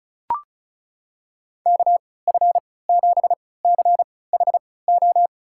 Morse code, heard as a keyed tone, about 36 wpm, spaced as for 20 wpm, 700 Hz. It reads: KF7CHO
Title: USA Call Signs - 6 Characters - 3x Character Spacing - 35wpm